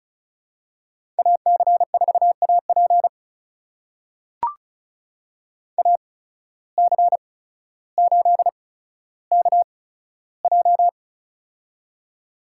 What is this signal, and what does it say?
Morse code, heard as a keyed tone, about 36 wpm, spaced as for 15 wpm, 700 Hz. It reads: AC4AP AC8KJ
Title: USA Call Signs - 5 Characters - 8x Character Spacing - 35wpm